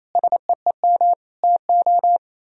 Morse code, heard as a keyed tone, about 36 wpm, 700 Hz. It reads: SEEM TO